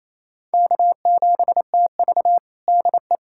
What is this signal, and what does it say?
Morse code, heard as a keyed tone, about 28 wpm, 700 Hz. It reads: K7TV DE
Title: USA Callsign DE Callsign - 4 Characters - 28wpm